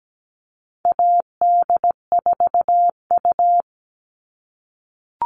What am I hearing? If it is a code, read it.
Morse code, heard as a keyed tone, about 17 wpm, 700 Hz. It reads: AD4U
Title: All Cut Numbers - 4-Digit 17wpm